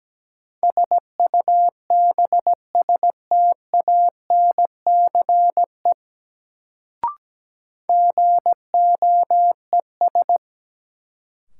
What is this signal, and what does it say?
Morse code, heard as a keyed tone, about 17 wpm, 700 Hz. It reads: SUBSTANCE GOES
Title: Sentences from Top 1500 Words - New Word Drill Alt - 17wpm